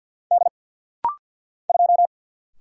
Morse code, heard as a keyed tone, about 50 wpm, 690 Hz. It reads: D 2
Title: Single Letter-Number Alt - Rapid-Fire 50wpm